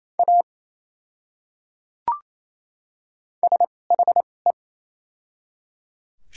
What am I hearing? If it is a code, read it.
Morse code, heard as a keyed tone, about 28 wpm, spaced as for 5 wpm, 700 Hz. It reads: AS HE
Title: 2x Character Spacing - Top 100 Words - 28wpm